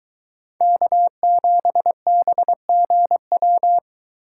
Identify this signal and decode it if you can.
Morse code, heard as a keyed tone, about 22 wpm, 700 Hz. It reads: K7BGW